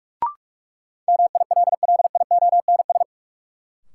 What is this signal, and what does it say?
Morse code, heard as a keyed tone, about 46 wpm, 700 Hz. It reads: MILLIONS
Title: Sets of 5 Words - New Word Drill - 45wpm